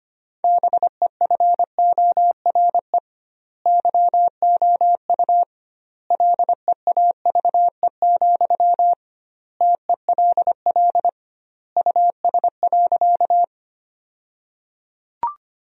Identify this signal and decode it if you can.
Morse code, heard as a keyed tone, about 25 wpm, 700 Hz. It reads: BEFORE YOU LEAVE, TELL US.
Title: 2x Word-Spacing - Sentences from Top 400 Words - 25wpm